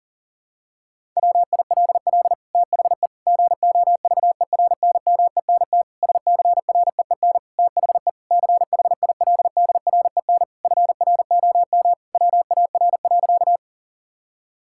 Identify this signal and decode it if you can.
Morse code, heard as a keyed tone, about 37 wpm, 700 Hz. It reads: WILL THE GOVERNMENT SCREEN THE CHILDREN FROM WAR.
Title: Sentences from Top 1000 Words - Review Part 6 - 40wpm